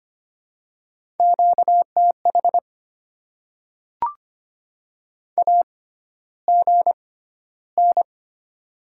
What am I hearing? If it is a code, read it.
Morse code, heard as a keyed tone, about 25 wpm, spaced as for 13 wpm, 700 Hz. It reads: QTH AGN
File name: Top 20 QSO Elements - 6x Character Spacing - 25wpm